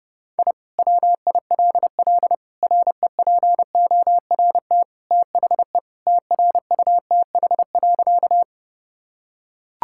Morse code, heard as a keyed tone, about 30 wpm, 700 Hz. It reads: I WILL REPORT THE TRUTH.
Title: Sentences from Top 1000 Words - Review Part 6 - 30wpm